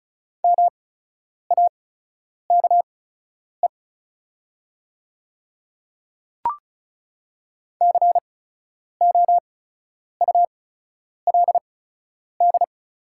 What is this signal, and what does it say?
Morse code, heard as a keyed tone, about 36 wpm, spaced as for 10 wpm, 690 Hz. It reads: MAKE COULD
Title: 8x Character Spacing - Top 500 Words - 35wpm